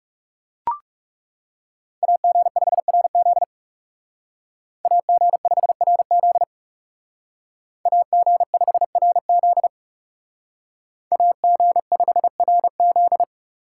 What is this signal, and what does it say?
Morse code, heard as a keyed tone, about 36 wpm, 700 Hz. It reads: AG5RZ AG5RZ AG5RZ AG5RZ
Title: Speed Racing - USA 2x2 Call Signs 30wpm